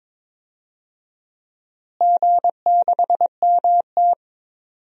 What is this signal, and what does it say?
Morse code, heard as a keyed tone, about 22 wpm, 700 Hz. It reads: G6MT
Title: DXCC-100 Call Signs - 4 Characters - Repeated 2x - 22wpm